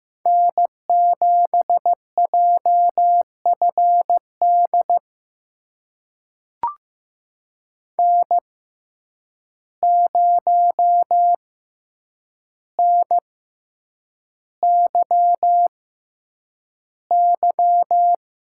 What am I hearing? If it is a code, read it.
Morse code, heard as a keyed tone, about 15 wpm, spaced as for 8 wpm, 700 Hz. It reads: N7JFD N0NYY
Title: USA Call Signs - 5 Characters - 6x Character Spacing - 15wpm